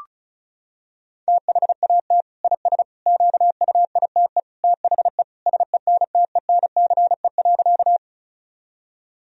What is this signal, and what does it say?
Morse code, heard as a keyed tone, about 34 wpm, 700 Hz. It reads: THAT IS QUITE THE SENTENCE.
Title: Sentences from Top 1000 Words - Review Part 1 - 35wpm